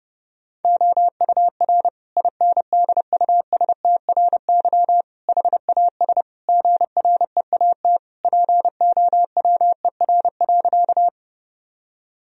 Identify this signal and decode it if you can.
Morse code, heard as a keyed tone, about 30 wpm, 700 Hz. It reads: OUR INDUSTRY HAS GREAT POWER.
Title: Sentences from Top 1000 Words - Review Part 4 - 30wpm